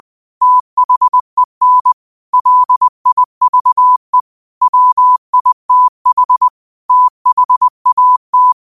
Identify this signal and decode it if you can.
Morse code, heard as a keyed tone, about 20 wpm, 1000 Hz. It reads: THEN LIVE WITH THAT
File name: qotdfiles / 20wpm